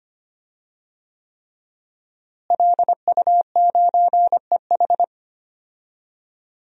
Morse code, heard as a keyed tone, about 25 wpm, 700 Hz. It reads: LU9EH
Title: DXCC-100 Call Signs - 5 Characters - Repeated 3x - 25wpm